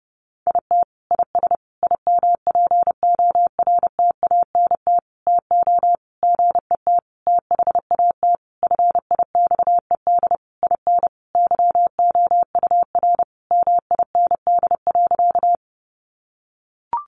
Morse code, heard as a keyed tone, about 30 wpm, 700 Hz. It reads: IT IS IMPORTANT TO GET THAT FIXED IN YOUR MIND.
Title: Sentences from Top 1100 Words 30wpm